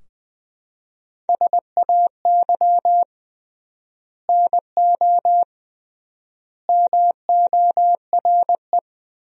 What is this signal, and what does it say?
Morse code, heard as a keyed tone, about 21 wpm, 700 Hz. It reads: SAY NO MORE
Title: 3x Word-Spacing - Sentences from Top 200 Words - 20wpm